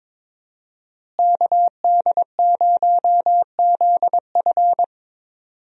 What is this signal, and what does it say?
Morse code, heard as a keyed tone, about 22 wpm, 700 Hz. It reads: KD0ZF
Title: USA Call Signs - 5 Characters - Repeated 3x - 22wpm